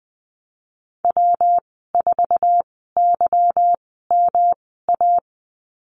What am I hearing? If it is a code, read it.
Morse code, heard as a keyed tone, about 19 wpm, spaced as for 14 wpm, 700 Hz. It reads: W4YMA
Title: 154 Lesson 25g - Letter p - Call Signs 20wpm